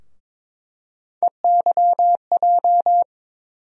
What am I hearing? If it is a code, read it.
Morse code, heard as a keyed tone, about 22 wpm, 700 Hz. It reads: EYJ
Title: USA Call Sign Postfixes - 3 Letters 22wpm